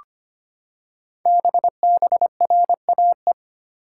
Morse code, heard as a keyed tone, about 25 wpm, 700 Hz. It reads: BBRAE
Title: Code Groups - 5 Letters - 25wpm